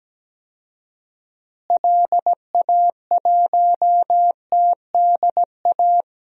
Morse code, heard as a keyed tone, about 17 wpm, 700 Hz. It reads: LA1TDA
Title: DXCC-100 Call Signs - 6 Characters - Repeated 2x - 17wpm